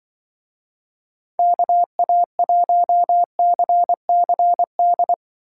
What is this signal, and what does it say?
Morse code, heard as a keyed tone, about 24 wpm, 700 Hz. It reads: KA1CCD
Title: Speed Racing - USA 2x3 Call Signs 20wpm